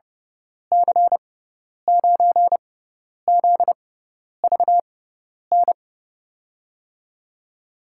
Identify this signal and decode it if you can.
Morse code, heard as a keyed tone, about 30 wpm, spaced as for 10 wpm, 700 Hz. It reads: C9ZVN
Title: USA Call Signs - 6x Character Spacing - 30wpm